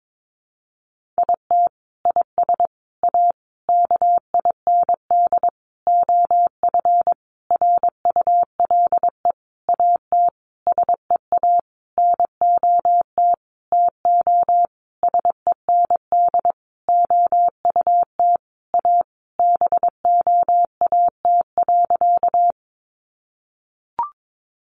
Morse code, heard as a keyed tone, about 22 wpm, 700 Hz. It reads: IT IS A KIND OF RULE AT SEA NOT TO SEND OUT A BOAT.
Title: Sentences from Top 1500 Words 22wpm